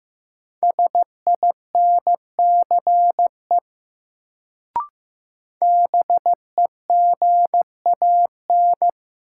Morse code, heard as a keyed tone, about 16 wpm, 700 Hz. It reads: SINCE BEGAN
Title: Sentences from Top 300 Words - New Word Drill - 15wpm